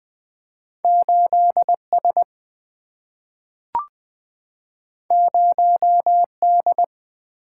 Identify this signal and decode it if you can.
Morse code, heard as a keyed tone, about 20 wpm, 700 Hz. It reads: 8 S 0 D
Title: Single-Letter Single-Number 20wpm